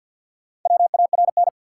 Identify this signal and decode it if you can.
Morse code, heard as a keyed tone, about 49 wpm, 700 Hz. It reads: WARN
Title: Four Letter Words - 50wpm